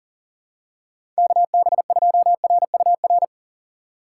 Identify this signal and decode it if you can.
Morse code, heard as a keyed tone, about 39 wpm, 700 Hz. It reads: KB2RUR
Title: USA Call Signs - 6 Characters - 3x Character Spacing - 40wpm